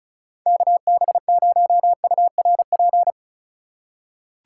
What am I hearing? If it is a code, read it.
Morse code, heard as a keyed tone, about 36 wpm, 700 Hz. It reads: KB0URP
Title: USA Call Signs - 6 Characters - 8x Character Spacing - 35wpm